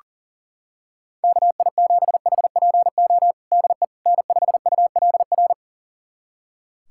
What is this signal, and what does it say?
Morse code, heard as a keyed tone, about 40 wpm, 700 Hz. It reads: KI7HPO DE N5ULR
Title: USA Callsign DE Callsign 40wpm